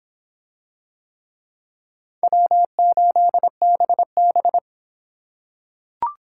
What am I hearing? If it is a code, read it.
Morse code, heard as a keyed tone, about 26 wpm, 700 Hz. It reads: W8BB